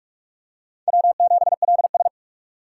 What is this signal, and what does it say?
Morse code, heard as a keyed tone, about 47 wpm, 700 Hz. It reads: W7LS
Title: USA Call Signs - 4 Characters - 3x Character Spacing - 45wpm